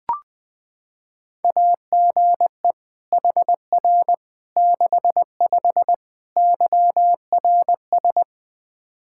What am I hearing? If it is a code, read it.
Morse code, heard as a keyed tone, about 20 wpm, 700 Hz. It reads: AGE HR 65 YRS
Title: Practice QSO - Exchange 3 - More Info - 20wpm